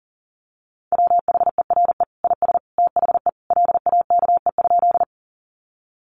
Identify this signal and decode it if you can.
Morse code, heard as a keyed tone, about 40 wpm, 700 Hz. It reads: WHERE IS THE LAKE?
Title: Sentences from Top 1100 Words 40wpm